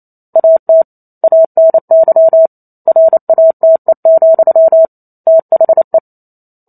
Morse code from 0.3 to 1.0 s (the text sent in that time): AT